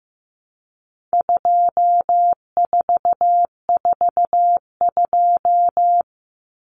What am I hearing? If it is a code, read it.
Morse code, heard as a keyed tone, about 15 wpm, 700 Hz. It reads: 2442
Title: Numbers - 4-Digit 15wpm